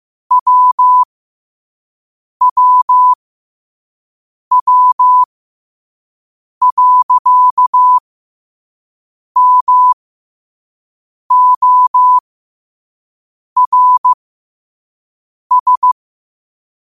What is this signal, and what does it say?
Morse code, heard as a keyed tone, about 15 wpm, spaced as for 5 wpm, 1000 Hz. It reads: WWW.MORS